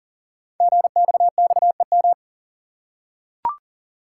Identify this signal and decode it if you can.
Morse code, heard as a keyed tone, about 42 wpm, 700 Hz. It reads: GXXEM
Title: Code Groups - 5 Letters - 40wpm